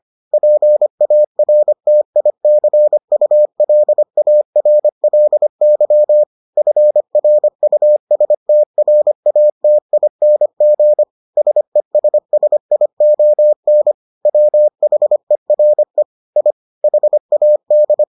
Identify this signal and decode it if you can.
Morse code, heard as a keyed tone, about 25 wpm, 600 Hz. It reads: PARTICULARLY FRUSTRATING SESSION WHERE I HAD